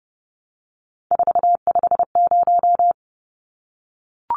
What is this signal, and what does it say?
Morse code, heard as a keyed tone, about 30 wpm, 700 Hz. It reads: 450
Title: Numbers - 3-Digit 30wpm